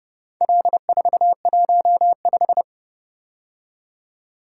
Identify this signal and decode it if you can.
Morse code, heard as a keyed tone, about 30 wpm, 700 Hz. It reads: L415